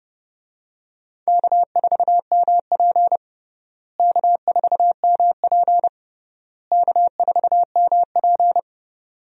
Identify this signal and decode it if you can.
Morse code, heard as a keyed tone, about 30 wpm, 700 Hz. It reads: K4MP K4MP K4MP
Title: USA Call Signs - 4 Characters - Repeated 3x - 30wpm